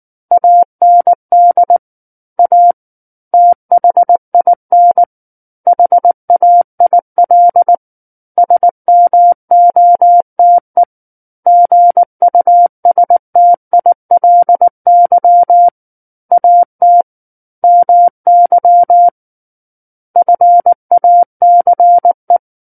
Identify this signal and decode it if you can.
Morse code, heard as a keyed tone, about 19 wpm, 700 Hz. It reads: AND A THIN HAIL SMOTE GUSTILY AT MY FACE
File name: Book One / 10-Chapter 10 - 19 WPM